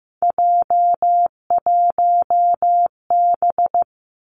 Morse code, heard as a keyed tone, about 15 wpm, 700 Hz. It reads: J1B
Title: Three Challenging Letter-Number Combinations 15wpm